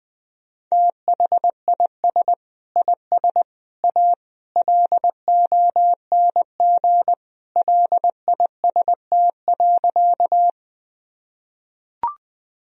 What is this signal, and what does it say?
Morse code, heard as a keyed tone, about 20 wpm, 700 Hz. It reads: THIS IS A LONG LIST.
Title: Sentences from Top 1000 Words - Review Part 2 - 20wpm